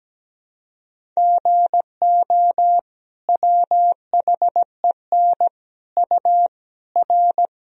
Morse code, heard as a keyed tone, about 17 wpm, 700 Hz. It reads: GO WHEN U R
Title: Sentences from Top 500 Words and Abbreviations 17wpm